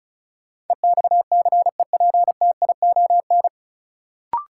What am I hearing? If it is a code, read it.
Morse code, heard as a keyed tone, about 36 wpm, 700 Hz. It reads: EXCEPTION
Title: Sets of 4 Words - New Word Drill - 35wpm